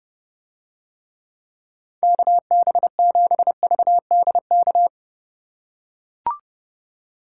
Morse code, heard as a keyed tone, about 31 wpm, 700 Hz. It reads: KB7VDK